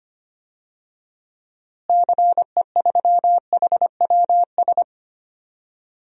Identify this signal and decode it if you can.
Morse code, heard as a keyed tone, about 25 wpm, 700 Hz. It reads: CE3HWS
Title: DXCC-100 Call Signs - 6 Characters - Repeated 3x - 25wpm